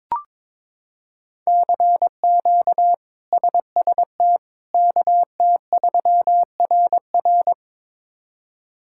Morse code, heard as a keyed tone, about 22 wpm, 700 Hz. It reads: CQ SST KT3RR
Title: K1USN SST QSO Practice - 22wpm